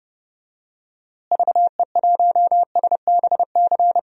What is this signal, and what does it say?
Morse code, heard as a keyed tone, about 30 wpm, 700 Hz. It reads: VE1SBC
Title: DXCC-100 Call Signs - 6 Characters - Repeated 2x - 30wpm